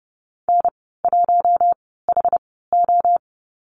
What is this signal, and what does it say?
Morse code, heard as a keyed tone, about 30 wpm, spaced as for 17 wpm, 700 Hz. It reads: N1HO
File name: Lesson 15g - Number 5 - Call Signs 30wpm